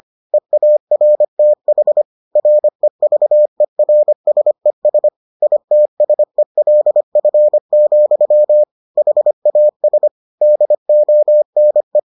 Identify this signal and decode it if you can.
Morse code, heard as a keyed tone, about 25 wpm, 600 Hz. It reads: EARTH REVERSES ITSELF, HAS DONE